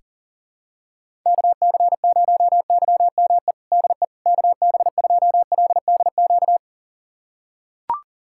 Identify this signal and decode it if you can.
Morse code, heard as a keyed tone, about 40 wpm, 700 Hz. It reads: KC0YME DE KB2LDQ
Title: USA Callsign DE Callsign - 6 Characters - 40wpm